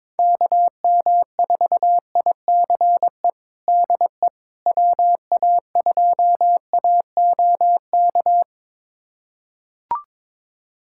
Morse code, heard as a keyed tone, about 22 wpm, 700 Hz. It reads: KM4ICE DE WA2AOK
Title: USA Callsign DE Callsign - 6 Characters - 22wpm